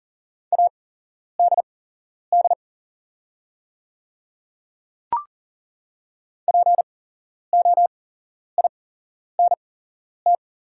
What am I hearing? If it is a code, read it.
Morse code, heard as a keyed tone, about 40 wpm, spaced as for 11 wpm, 690 Hz. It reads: ADD POINT
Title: 8x Character Spacing - Top 200 Words - 40wpm